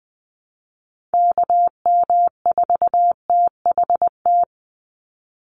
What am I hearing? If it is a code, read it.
Morse code, heard as a keyed tone, about 20 wpm, 700 Hz. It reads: KM4THT
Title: USA Call Signs 20wpm